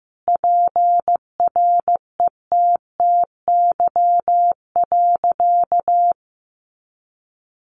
Morse code, heard as a keyed tone, about 15 wpm, 700 Hz. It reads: PRETTY.
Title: Sentences from Top 1000 Words 15wpm